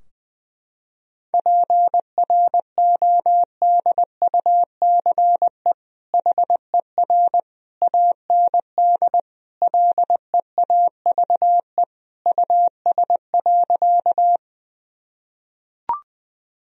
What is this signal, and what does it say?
Morse code, heard as a keyed tone, about 20 wpm, 700 Hz. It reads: PRODUCE HER AND LEAVE US.
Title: Sentences from Top 1000 Words - Review Part 3 - 20wpm